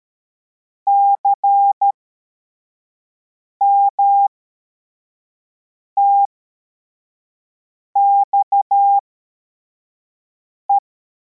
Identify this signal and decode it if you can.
Morse code, heard as a keyed tone, about 13 wpm, spaced as for 4 wpm, 800 Hz. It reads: CMTXE